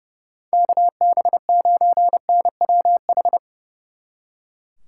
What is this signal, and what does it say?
Morse code, heard as a keyed tone, about 40 wpm, spaced as for 35 wpm, 700 Hz. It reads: KB9NWH
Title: Speed Racing - USA 2x3 Call Signs 30wpm